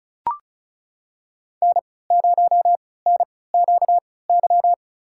Morse code, heard as a keyed tone, about 36 wpm, spaced as for 20 wpm, 700 Hz. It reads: N0NQY
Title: USA Call Signs - 5 Characters - 3x Character Spacing - 35wpm